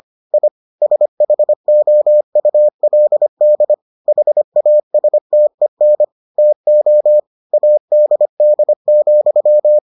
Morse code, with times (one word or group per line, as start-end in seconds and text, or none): 0.3-0.5 I
0.8-3.7 SHOULD
4.1-6.0 HASTEN
6.4-7.2 TO
7.5-9.8 ADD,